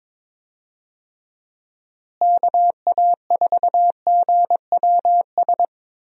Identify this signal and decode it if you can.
Morse code, heard as a keyed tone, about 22 wpm, 700 Hz. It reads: KA4GWS
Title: USA Call Signs - 6 Characters - Repeated 2x - 22wpm